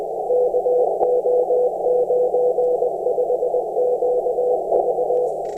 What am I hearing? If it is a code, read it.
Morse code, heard as a keyed tone, about 20 wpm, 500 Hz, with noise. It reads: KO95QV
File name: attachments / 16.23.03 r2ajab